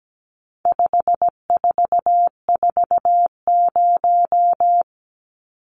Morse code, heard as a keyed tone, about 17 wpm, 700 Hz. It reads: 5440
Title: Numbers - 4-Digit 17wpm